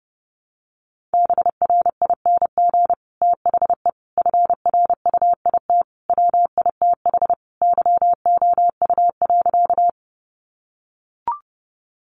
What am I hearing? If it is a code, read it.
Morse code, heard as a keyed tone, about 28 wpm, 700 Hz. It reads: BRING THE FRUIT WITH YOU.
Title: Sentences from Top 1300 Words 30wpm